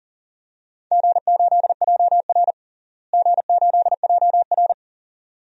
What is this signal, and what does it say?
Morse code, heard as a keyed tone, about 40 wpm, 700 Hz. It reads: G8JR G8JR
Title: DXCC-100 Call Signs - 4 Characters - Repeated 2x - 40wpm